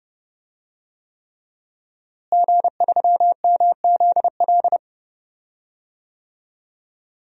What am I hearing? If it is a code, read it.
Morse code, heard as a keyed tone, about 30 wpm, 700 Hz. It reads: G3MZL
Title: DXCC-100 Call Signs - 5 Characters - Repeated 3x - 30wpm